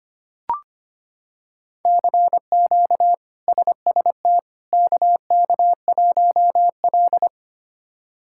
Morse code, heard as a keyed tone, about 25 wpm, 700 Hz. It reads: CQ SST KK1L
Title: K1USN SST QSO Practice - 25wpm